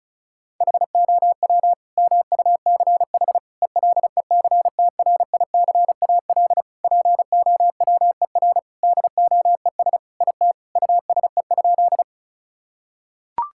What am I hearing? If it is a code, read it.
Morse code, heard as a keyed tone, about 35 wpm, 700 Hz. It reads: HOW MUCH ELECTRICAL POWER DOES IT USE?
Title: Sentences from Top 1000 Words - Review Part 6 - 35wpm